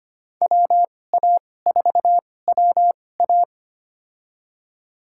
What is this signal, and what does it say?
Morse code, heard as a keyed tone, about 25 wpm, spaced as for 18 wpm, 700 Hz. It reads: WA4WA
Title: USA Call Signs - 5 Characters - 2x Character Spacing - 25wpm